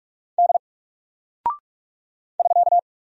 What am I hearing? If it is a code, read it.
Morse code, heard as a keyed tone, about 46 wpm, 700 Hz. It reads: D <SK>
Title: Single Character (Letter:Number:Punctuation:Prosign) 45wpm